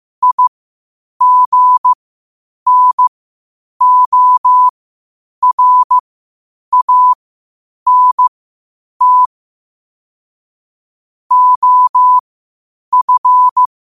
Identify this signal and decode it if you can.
Morse code, heard as a keyed tone, about 15 wpm, spaced as for 8 wpm, 1000 Hz. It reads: IGNORANT OF
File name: qotdfiles / 8wpm